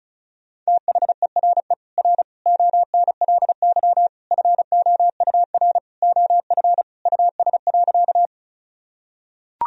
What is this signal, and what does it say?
Morse code, heard as a keyed tone, about 36 wpm, 700 Hz. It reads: THERE R ONLY FOUR OF US.
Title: Sentences from Top 300 Words and Abbreviations 35wpm